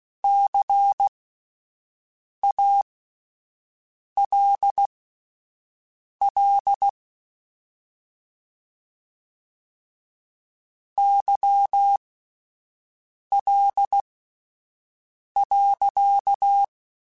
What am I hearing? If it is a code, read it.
Morse code, heard as a keyed tone, about 16 wpm, spaced as for 5 wpm, 800 Hz. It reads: CALL YL.
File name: (spaced) / QSOs Spaced-mp3s / WA7QOX 2 @10 sp